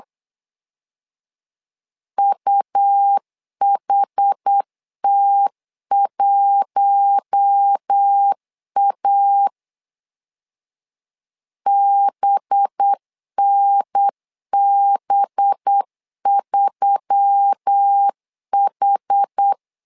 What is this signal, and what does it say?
Morse code, heard as a keyed tone, about 8 wpm, 800 Hz. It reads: UHT1A BNB3H
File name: LEZIONI / 5 LBVC / 5 LB 8wpm 2816